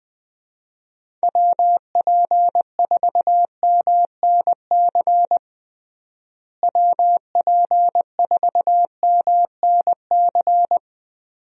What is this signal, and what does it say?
Morse code, heard as a keyed tone, about 20 wpm, 700 Hz. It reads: WP4MNC WP4MNC
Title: USA Call Signs - 6 Characters - Repeated 2x - 20wpm